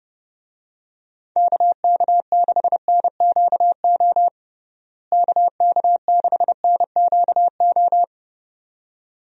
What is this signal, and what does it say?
Morse code, heard as a keyed tone, about 30 wpm, 700 Hz. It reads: KK6NQO KK6NQO
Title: USA Call Signs - 6 Characters - Repeated 2x - 30wpm